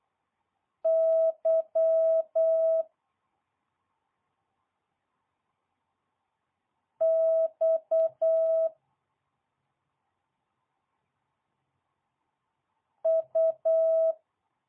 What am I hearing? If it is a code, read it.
Morse code, heard as a keyed tone, about 8 wpm, 650 Hz. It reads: Y X U
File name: jh8cbh / 2022cw